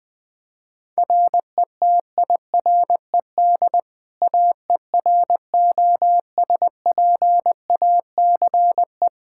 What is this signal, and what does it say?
Morse code, heard as a keyed tone, about 20 wpm, 700 Hz. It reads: RETIRED AEROSPACE
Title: K1USN SST QSO Practice 20wpm